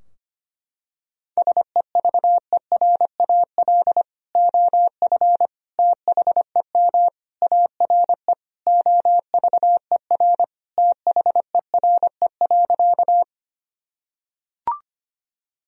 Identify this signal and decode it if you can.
Morse code, heard as a keyed tone, about 25 wpm, 700 Hz. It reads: SEVERAL OF THEM ARE OVER THERE.
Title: Sentences from Top 1000 Words - Review Part 2 - 25wpm